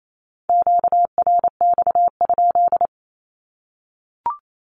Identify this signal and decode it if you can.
Morse code, heard as a keyed tone, about 28 wpm, 700 Hz. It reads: QRX?
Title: Q-Codes Repeated 3x 28wpm